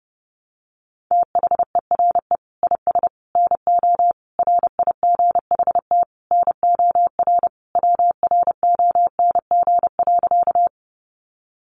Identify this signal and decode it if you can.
Morse code, heard as a keyed tone, about 30 wpm, 700 Hz. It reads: THERE IS NO RIGHT NOR WRONG.
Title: Sentences from Top 1500 Words 30wpm